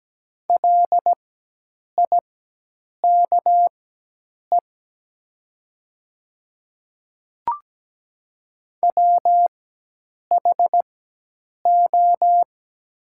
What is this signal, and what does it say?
Morse code, heard as a keyed tone, about 17 wpm, spaced as for 8 wpm, 700 Hz. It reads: LIKE WHO
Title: 4x Character Spacing - Top 100 Words - 17wpm